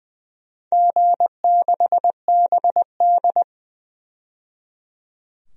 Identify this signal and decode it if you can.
Morse code, heard as a keyed tone, about 21 wpm, 700 Hz. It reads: G6BD
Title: DXCC-100 Call Signs - 4 Characters - 20wpm